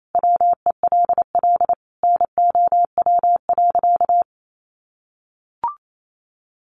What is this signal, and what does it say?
Morse code, heard as a keyed tone, about 28 wpm, 700 Hz. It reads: WELL NOW.
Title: Sentences from Top 1000 Words 28wpm